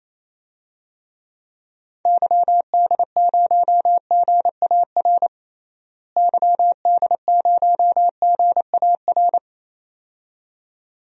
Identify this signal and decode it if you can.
Morse code, heard as a keyed tone, about 28 wpm, 700 Hz. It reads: YD0GAR YD0GAR
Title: DXCC-100 Call Signs - 6 Characters - Repeated 2x - 28wpm